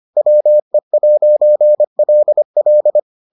Morse code, heard as a keyed tone, about 25 wpm, 600 Hz. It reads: WE'LL